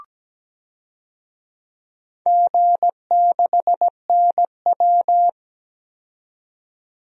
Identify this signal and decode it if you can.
Morse code, heard as a keyed tone, about 17 wpm, 700 Hz. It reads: G6NW
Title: DXCC-100 Call Signs - 4 Characters - 17wpm